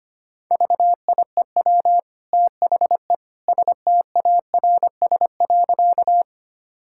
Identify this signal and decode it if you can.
Morse code, heard as a keyed tone, about 25 wpm, 700 Hz. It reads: VIEW THE STARS.
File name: Easy Sentences with Hints - 25wpm